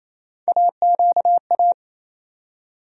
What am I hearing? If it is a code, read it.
Morse code, heard as a keyed tone, about 28 wpm, 700 Hz. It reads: AQA
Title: USA Call Sign Postfixes - 3 Letters 28wpm